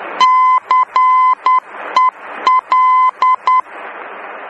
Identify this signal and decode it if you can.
Morse code, heard as a keyed tone, about 10 wpm, 1020 Hz. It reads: CEL